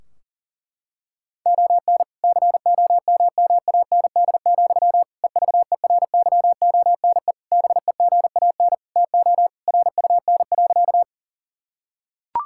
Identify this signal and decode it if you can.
Morse code, heard as a keyed tone, about 40 wpm, 700 Hz. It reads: ON COMMAND, EVERYONE BEGAN TO RUN.
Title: Sentences from Top 1600 Words 40wpm